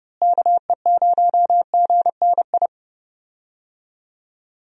Morse code, heard as a keyed tone, about 31 wpm, 700 Hz. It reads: KE0GNI
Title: USA Call Signs - 6 Characters - Repeated 2x - 30wpm